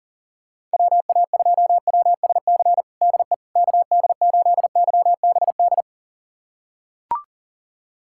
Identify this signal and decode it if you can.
Morse code, heard as a keyed tone, about 40 wpm, 700 Hz. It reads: WA2WSC DE KD8YBD
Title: USA Callsign DE Callsign - 6 Characters - 40wpm